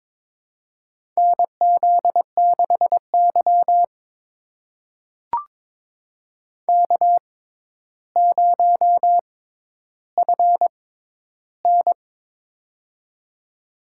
Morse code, heard as a keyed tone, about 22 wpm, spaced as for 11 wpm, 700 Hz. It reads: NZ6Y K0FN
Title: USA Call Signs - 6x Character Spacing - 22wpm